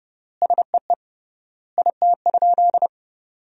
Morse code, heard as a keyed tone, about 32 wpm, 700 Hz. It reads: SEE IT?
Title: 3x Word-Spacing - Sentences from Top 100 Words - 30wpm